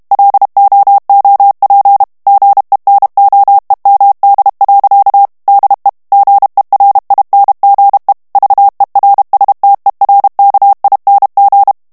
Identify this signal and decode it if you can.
Morse code, heard as a keyed tone, about 32 wpm, 800 Hz. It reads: LOOP GENOEMD. DE GERINGE VERSTERKING
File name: HOLLANDAIS texte clair 1 vitesse 30 mots minute